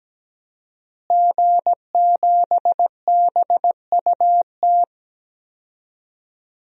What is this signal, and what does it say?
Morse code, heard as a keyed tone, about 17 wpm, 700 Hz. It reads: G7BUT
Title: DXCC-100 Call Signs - 5 Characters - Repeated 3x - 17wpm